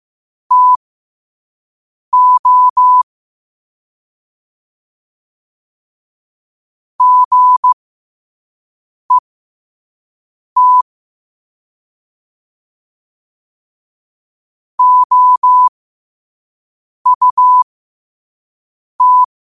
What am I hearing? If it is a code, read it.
Morse code, heard as a keyed tone, about 15 wpm, spaced as for 5 wpm, 1000 Hz. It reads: TO GET OUT
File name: qotdfiles / 5wpm